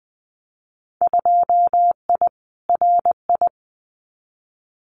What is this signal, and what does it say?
Morse code, heard as a keyed tone, about 20 wpm, 700 Hz. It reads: 2I RI
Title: Winter Field Day Exchanges Encore 20wpm